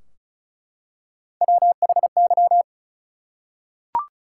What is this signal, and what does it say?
Morse code, heard as a keyed tone, about 34 wpm, 700 Hz. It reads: WHY
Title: Three Letter Words - 35wpm